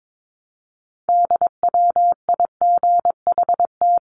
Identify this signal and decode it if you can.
Morse code, heard as a keyed tone, about 22 wpm, 700 Hz. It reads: DWIGHT
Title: CWT Exchange 22wpm